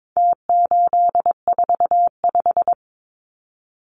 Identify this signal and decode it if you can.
Morse code, heard as a keyed tone, about 22 wpm, 700 Hz. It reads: T845
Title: Numbers - Contest Style - 4-Digit 22wpm